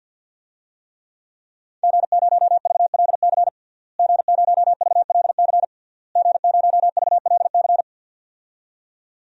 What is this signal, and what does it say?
Morse code, heard as a keyed tone, about 51 wpm, 700 Hz. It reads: G0VLC G0VLC G0VLC
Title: DXCC-100 Call Signs - 5 Characters - Repeated 3x - 50wpm